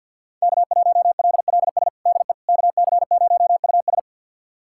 Morse code, heard as a keyed tone, about 47 wpm, 690 Hz. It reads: K1LLS DE KC0US
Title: USA Callsign DE Callsign - 5 Characters - 50wpm